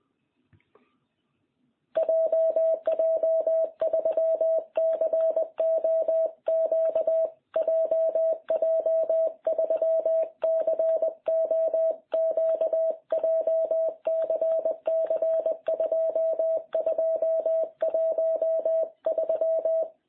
Morse code, heard as a keyed tone, about 20 wpm, 660 Hz. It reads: JJ3COQ JJ3COQ JCC 2213